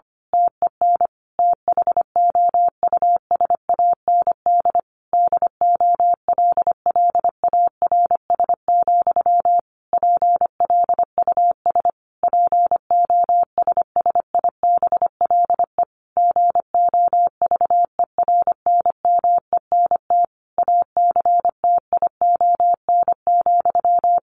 Morse code, heard as a keyed tone, about 25 wpm, 700 Hz. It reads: TEN THOUSAND DOLLARS, PLUS POSSIBLE GOVERNMENT ACTION,
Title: A Sound Of Thunder 25wpm